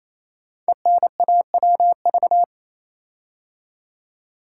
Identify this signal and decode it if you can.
Morse code, heard as a keyed tone, about 27 wpm, 700 Hz. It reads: ENAWV
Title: Code Groups - 5 Letters - 28wpm